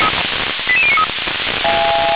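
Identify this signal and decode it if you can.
Five-tone selective call (ZVEI), heard as quick runs of five tones, tones of about 70 ms. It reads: 88093
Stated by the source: Five-tone ZVEI selective call received by SDR.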